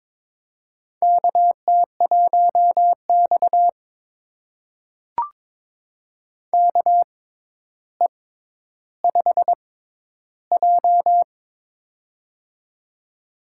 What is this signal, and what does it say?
Morse code, heard as a keyed tone, about 22 wpm, spaced as for 11 wpm, 700 Hz. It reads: KT1X KE5J
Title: USA Call Signs - 4 Characters - 6x Character Spacing - 22wpm